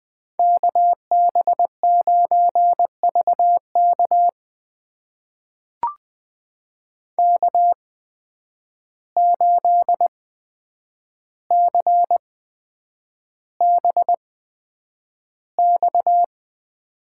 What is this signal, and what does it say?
Morse code, heard as a keyed tone, about 20 wpm, spaced as for 9 wpm, 700 Hz. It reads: KB9VK K8CBX
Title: USA Call Signs - 5 Characters - 8x Character Spacing - 20wpm